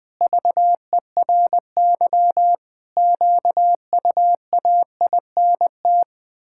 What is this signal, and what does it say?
Morse code, heard as a keyed tone, about 20 wpm, 700 Hz. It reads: VERY QUAINT
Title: Pangram Challenge v2 - 20wpm